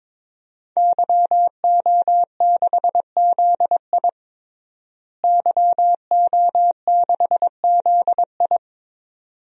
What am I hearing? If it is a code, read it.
Morse code, heard as a keyed tone, about 21 wpm, 700 Hz. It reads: YO6ZI YO6ZI
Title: DXCC-100 Call Signs - 5 Characters - Repeated 2x - 22wpm